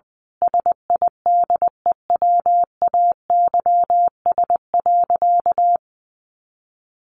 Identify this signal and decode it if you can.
Morse code, heard as a keyed tone, about 20 wpm, 700 Hz. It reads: SIDEWAYS.
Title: A Sound Of Thunder 20wpm